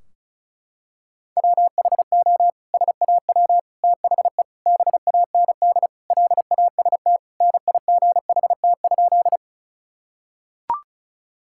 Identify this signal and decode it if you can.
Morse code, heard as a keyed tone, about 36 wpm, 700 Hz. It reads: WHO SAW THE BAND LAST NIGHT?
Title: Sentences from Top 1600 Words 35wpm